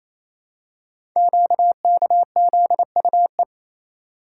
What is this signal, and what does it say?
Morse code, heard as a keyed tone, about 27 wpm, 700 Hz. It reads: QKZUE
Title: Code Groups - 5 Letters - 28wpm